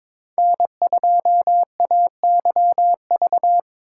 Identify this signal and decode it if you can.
Morse code, heard as a keyed tone, about 22 wpm, 700 Hz. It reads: N2AYV
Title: USA Call Signs - 5 Characters - 2x Character Spacing - 22wpm